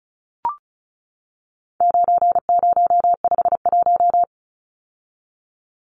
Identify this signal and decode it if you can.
Morse code, heard as a keyed tone, about 34 wpm, 700 Hz. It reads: 9051
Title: Numbers - 4-Digit 35wpm